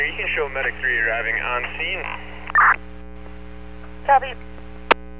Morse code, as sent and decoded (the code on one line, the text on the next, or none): none